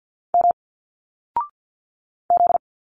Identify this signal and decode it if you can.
Morse code, heard as a keyed tone, about 50 wpm, 700 Hz. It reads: M Z
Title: Single Letters 50wpm